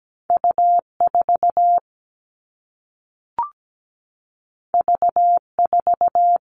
Morse code, heard as a keyed tone, about 17 wpm, 700 Hz. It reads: U4 V4
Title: All Cut Numbers - 2-Digit 17wpm